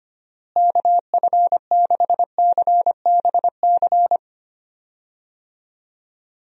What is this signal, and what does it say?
Morse code, heard as a keyed tone, about 25 wpm, 700 Hz. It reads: KF6CBC
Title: USA Call Signs - 6 Characters - Repeated 2x - 25wpm